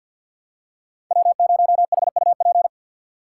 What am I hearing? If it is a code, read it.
Morse code, heard as a keyed tone, about 53 wpm, 700 Hz. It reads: W0HUP
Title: USA Call Signs - 5 Characters - 4x Character Spacing - 50wpm